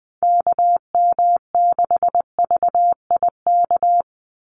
Morse code, heard as a keyed tone, about 20 wpm, 700 Hz. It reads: KM6VIK
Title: USA Call Signs 20wpm